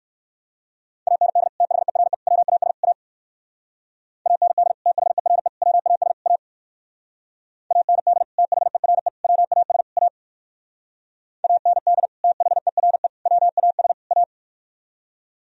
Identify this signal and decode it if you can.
Morse code, heard as a keyed tone, about 58 wpm, 700 Hz. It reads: AND THERE WAS A AND THERE WAS A AND THERE WAS A AND THERE WAS A